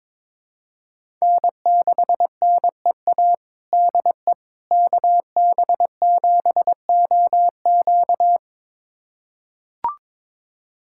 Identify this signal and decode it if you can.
Morse code, heard as a keyed tone, about 22 wpm, 700 Hz. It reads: N6NEA DE KB7OQ
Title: USA Callsign DE Callsign - 5 Characters - 22wpm